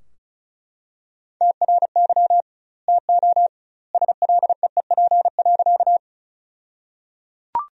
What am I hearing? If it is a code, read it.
Morse code, heard as a keyed tone, about 34 wpm, 700 Hz. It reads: TRY TO SLEEP.
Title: 2x Word-Spacing - Sentences from Top 500 Words - 35wpm